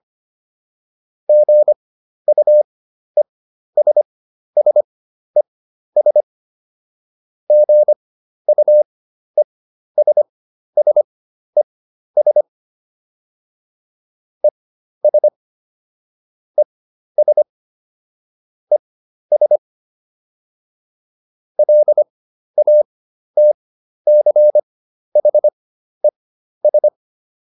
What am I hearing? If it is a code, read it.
Morse code, heard as a keyed tone, about 25 wpm, spaced as for 12 wpm, 600 Hz. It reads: GUESSES GUESSES ES ES ES LATCHES